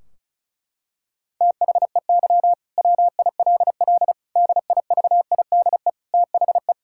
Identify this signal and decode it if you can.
Morse code, heard as a keyed tone, about 36 wpm, 700 Hz. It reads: THEY WILL DIVIDE THE
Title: Sentences from Top 1600 Words 35wpm